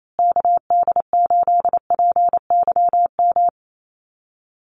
Morse code, heard as a keyed tone, about 28 wpm, 700 Hz. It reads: KD8PYM
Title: USA Call Signs Repeated 2x 28wpm